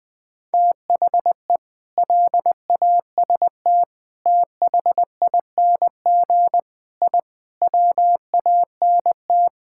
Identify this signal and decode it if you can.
Morse code, heard as a keyed tone, about 19 wpm, 700 Hz. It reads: THE LAST THING I WANT
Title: Sentences with 7-Word Phrases - Spoken Beginning - 20wpm